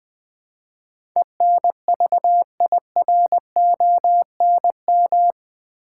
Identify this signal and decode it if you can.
Morse code, heard as a keyed tone, about 20 wpm, 700 Hz. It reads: ENVIRONM
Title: Build A Word - 4 Letters At A Time - 11-Letter Words - 20wpm